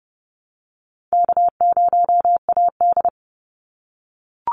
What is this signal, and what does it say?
Morse code, heard as a keyed tone, about 30 wpm, 700 Hz. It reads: K0AD
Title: Easy Call Signs - 30wpm